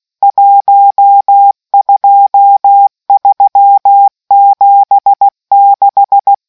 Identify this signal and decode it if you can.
Morse code, heard as a keyed tone, about 16 wpm, 800 Hz. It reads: 12376